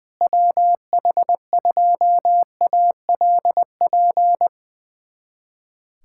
Morse code, heard as a keyed tone, about 20 wpm, 700 Hz. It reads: WH2ALP
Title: USA Callsign DE Callsign - 6 Characters - 20wpm